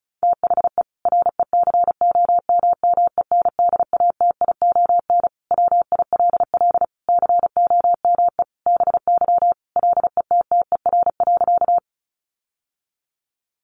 Morse code, heard as a keyed tone, about 34 wpm, 700 Hz. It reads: THE RECOMMENDATION WILL COME BY LETTER.
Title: Sentences from Top 1400 Words 35wpm